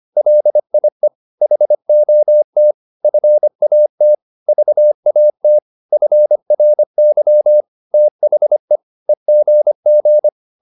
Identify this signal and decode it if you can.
Morse code, heard as a keyed tone, about 25 wpm, 600 Hz. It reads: LIE HOT FAT VAT FRY THE EGG